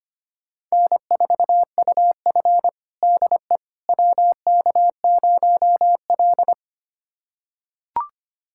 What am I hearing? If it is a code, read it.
Morse code, heard as a keyed tone, about 25 wpm, 700 Hz. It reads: N4UF DE WK0L
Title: USA Callsign DE Callsign - 4 Characters - 25wpm